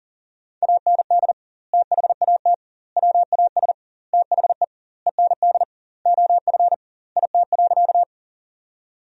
Morse code, heard as a keyed tone, about 37 wpm, 700 Hz. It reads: AND THAT WAS THE END OF IT.
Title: 2x Word-Spacing - Sentences from Top 200 Words - 40wpm